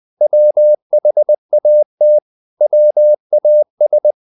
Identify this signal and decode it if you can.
Morse code, heard as a keyed tone, about 20 wpm, 600 Hz. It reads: WHAT WAS
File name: the old man and the sea part 2 20wpm